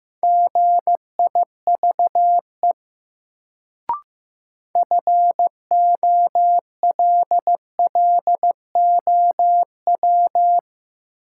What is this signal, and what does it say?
Morse code, heard as a keyed tone, about 15 wpm, 700 Hz. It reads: GIVE FOLLOW
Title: Sentences from Top 200 Words - New Word Drill - 15wpm